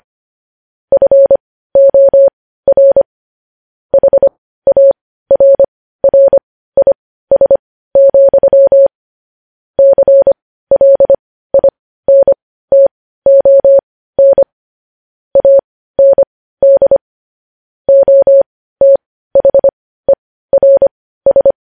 Morse code, heard as a keyed tone, about 25 wpm, spaced as for 15 wpm, 550 Hz. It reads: FOR HARRIS, CLINTON AND OTHERS
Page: news headlines in morse code at 15 wpm